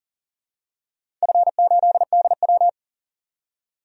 Morse code, heard as a keyed tone, about 40 wpm, 700 Hz. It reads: F8DW
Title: DXCC-100 Call Signs - 4 Characters - 40wpm